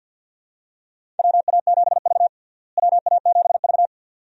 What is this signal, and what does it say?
Morse code, heard as a keyed tone, about 53 wpm, 700 Hz. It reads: WA7V WA7V
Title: USA Call Signs - 4 Characters - Repeated 2x - 50wpm